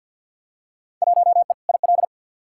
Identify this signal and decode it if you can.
Morse code, heard as a keyed tone, about 53 wpm, 700 Hz. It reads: 1E IL
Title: Field Day Exchanges - 50wpm